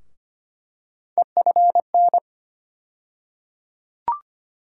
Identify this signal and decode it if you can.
Morse code, heard as a keyed tone, about 25 wpm, 700 Hz. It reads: EFN
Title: USA Call Sign Postfixes - 3 Letters 25wpm